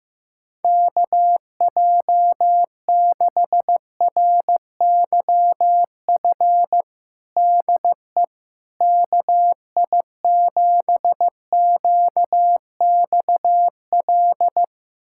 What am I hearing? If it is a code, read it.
Morse code, heard as a keyed tone, about 15 wpm, 700 Hz. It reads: KJ6RYF DE KI7QXL
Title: USA Callsign DE Callsign - 6 Characters - 15wpm